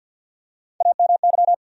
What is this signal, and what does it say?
Morse code, heard as a keyed tone, about 53 wpm, 700 Hz. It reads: AMY
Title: USA Common Names - Max 5 Letters - 50wpm